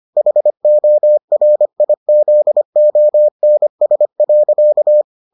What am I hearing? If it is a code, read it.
Morse code, heard as a keyed tone, about 25 wpm, 600 Hz. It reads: HORIZONS.